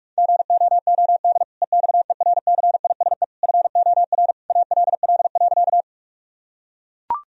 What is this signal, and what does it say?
Morse code, heard as a keyed tone, about 41 wpm, 700 Hz. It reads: GOOD EXERCISE FOR ALL.
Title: Sentences from Top 1000 Words - Review Part 4 - 45wpm